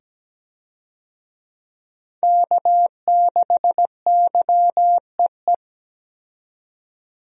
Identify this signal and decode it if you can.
Morse code, heard as a keyed tone, about 17 wpm, 700 Hz. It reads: K6YEE
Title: USA Call Signs - 5 Characters - 17wpm